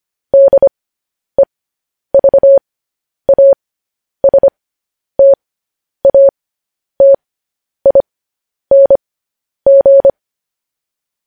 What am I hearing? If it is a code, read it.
Morse code, heard as a keyed tone, about 25 wpm, spaced as for 10 wpm, 550 Hz. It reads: DEVASTATING
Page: news-headlines-in-morse-code-at-10-wpm